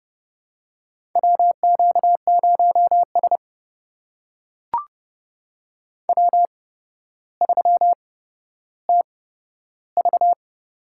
Morse code, heard as a keyed tone, about 30 wpm, spaced as for 13 wpm, 700 Hz. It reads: WQ0S W3TV
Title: USA Call Signs - 8x Character Spacing - 30wpm